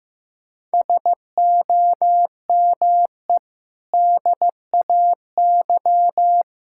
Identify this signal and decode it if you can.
Morse code, heard as a keyed tone, about 15 wpm, 700 Hz. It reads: SOME DAY
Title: Sentences from Top 1000 Words - Review Part 6 - 15wpm